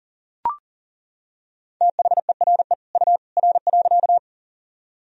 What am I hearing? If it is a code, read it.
Morse code, heard as a keyed tone, about 40 wpm, 700 Hz. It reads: THERE U R.
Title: Sentences from Top 100 Words and Abbreviations 40wpm